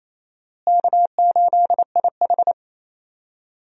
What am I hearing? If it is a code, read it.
Morse code, heard as a keyed tone, about 28 wpm, 700 Hz. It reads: K8IH
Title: USA Call Signs - 4 Characters - Repeated 2x - 28wpm